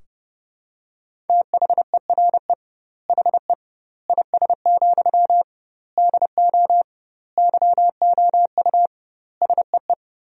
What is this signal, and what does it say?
Morse code, heard as a keyed tone, about 30 wpm, 700 Hz. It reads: THERE HE IS, DO YOU SEE
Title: 2x Word-Spacing - Sentences from Top 100 Words - 30wpm